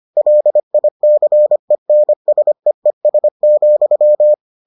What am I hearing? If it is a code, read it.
Morse code, heard as a keyed tone, about 25 wpm, 600 Hz. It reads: LICENSEES,